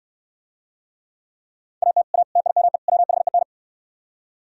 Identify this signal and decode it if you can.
Morse code, heard as a keyed tone, about 67 wpm, 700 Hz. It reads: AT A TIME WHEN